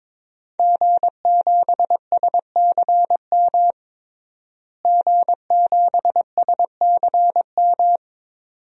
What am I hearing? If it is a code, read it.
Morse code, heard as a keyed tone, about 22 wpm, 700 Hz. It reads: G7SCM G7SCM
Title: DXCC-100 Call Signs - 5 Characters - Repeated 2x - 22wpm